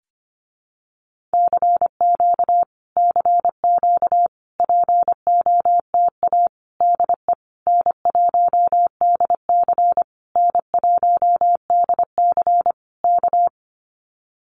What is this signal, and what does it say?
Morse code, heard as a keyed tone, about 25 wpm, 700 Hz. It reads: CQ CQ POTA DE N1DC N1DC K
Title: Common QSO Phrases Encore 25wpm